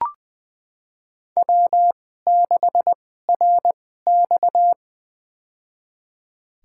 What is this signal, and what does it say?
Morse code, heard as a keyed tone, about 20 wpm, spaced as for 14 wpm, 700 Hz. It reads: W6RX
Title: USA Call Signs - 4 Characters - 2x Character Spacing - 20wpm